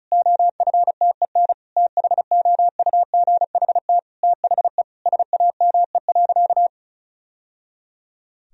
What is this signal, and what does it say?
Morse code, heard as a keyed tone, about 34 wpm, 700 Hz. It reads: OFTEN THOUGHT THE SAME.
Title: Sentences from Top 1000 Words - Review Part 1 - 35wpm